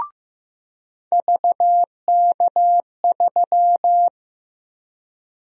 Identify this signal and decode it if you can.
Morse code, heard as a keyed tone, about 16 wpm, 700 Hz. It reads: VK3
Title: USA Call Sign Prefixes - 15wpm